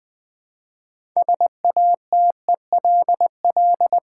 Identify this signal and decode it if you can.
Morse code, heard as a keyed tone, about 20 wpm, 700 Hz. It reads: SATELL
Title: Build A Word - 3 Letters At A Time - 9-Letter Words - 20wpm